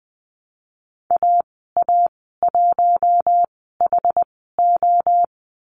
Morse code, heard as a keyed tone, about 20 wpm, spaced as for 14 wpm, 700 Hz. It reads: AA1HO
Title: 062 Lesson 11g - Letter h - Call Signs 20wpm